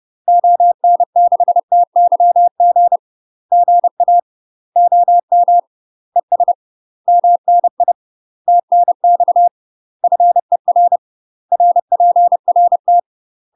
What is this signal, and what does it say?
Morse code, heard as a keyed tone, about 30 wpm, 700 Hz. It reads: ON6TYG GA OM ES MNI TNX FER RPRT
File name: UK MASTER QSOs TRAINING / 30WPM / UK MASTER QSOs 0007